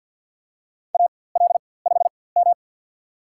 Morse code, heard as a keyed tone, about 53 wpm, spaced as for 24 wpm, 700 Hz. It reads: AL5M
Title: USA Call Signs - 4x Character Spacing - 50wpm